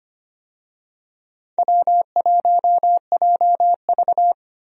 Morse code, heard as a keyed tone, about 25 wpm, 700 Hz. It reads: W1JV
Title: USA Call Signs - 4 Characters - 25wpm